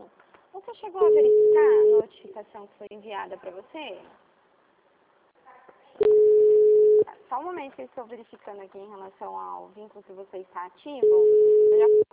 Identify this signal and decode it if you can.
Ringback tone (Europe, 425 Hz), with speech.
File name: out- -2024- - - .2480